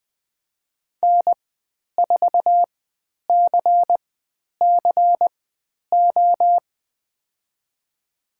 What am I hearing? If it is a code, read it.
Morse code, heard as a keyed tone, about 20 wpm, spaced as for 10 wpm, 700 Hz. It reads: N4CCO
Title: 102 Lesson 17g - Letter u - Call Signs 20wpm